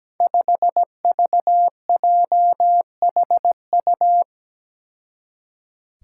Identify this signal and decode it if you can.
Morse code, heard as a keyed tone, about 17 wpm, 700 Hz. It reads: 5VJHU
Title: Five Challenging Letter-Number Combinations 17wpm